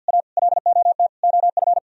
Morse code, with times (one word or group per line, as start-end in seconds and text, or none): 0.1-0.2 A
0.4-1.1 LOT
1.2-1.8 OF